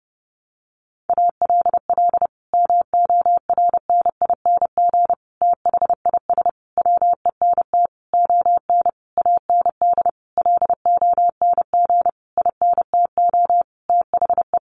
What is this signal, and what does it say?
Morse code, heard as a keyed tone, about 30 wpm, 700 Hz. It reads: ALL MORNING THIS WENT ON AND LONG INTO THE
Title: Sentences from Top 1500 Words 30wpm